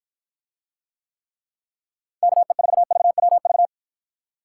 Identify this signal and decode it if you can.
Morse code, heard as a keyed tone, about 52 wpm, 700 Hz. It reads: KE4VWV